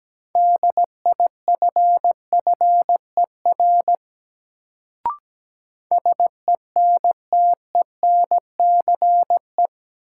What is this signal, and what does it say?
Morse code, heard as a keyed tone, about 17 wpm, 700 Hz. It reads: DIFFER SENTENCE
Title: Sentences from Top 1000 Words - Review Part 1 - New Word Drill - 17wpm